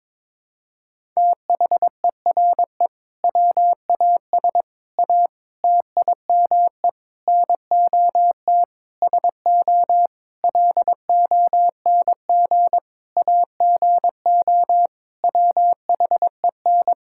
Morse code, heard as a keyed tone, about 22 wpm, 700 Hz. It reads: THERE WAS A TIME NOT SO LONG AGO WHEN
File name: practice / 22wpm / Sentences with 8-Word Phrases - 22wpm